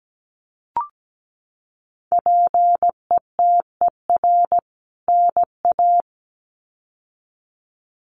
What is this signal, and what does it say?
Morse code, heard as a keyed tone, about 17 wpm, 700 Hz. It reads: PETER NA
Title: CWT Exchange 17wpm